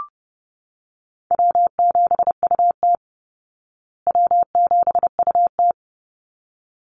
Morse code, heard as a keyed tone, about 29 wpm, 700 Hz. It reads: W7UT W7UT
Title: Easy Call Signs Repeated 2x - 30wpm